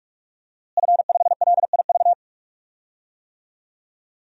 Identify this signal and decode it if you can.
Morse code, heard as a keyed tone, about 43 wpm, 700 Hz. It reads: F5LIV
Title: DXCC-100 Call Signs - 5 Characters - Repeated 3x - 45wpm